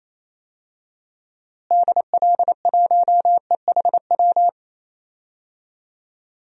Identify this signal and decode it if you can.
Morse code, heard as a keyed tone, about 28 wpm, 700 Hz. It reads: DL1EHW
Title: DXCC-100 Call Signs - 6 Characters - Repeated 2x - 28wpm